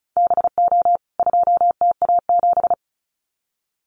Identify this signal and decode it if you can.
Morse code, heard as a keyed tone, about 34 wpm, 700 Hz. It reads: BO 2TA7
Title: CWT Exchange 35wpm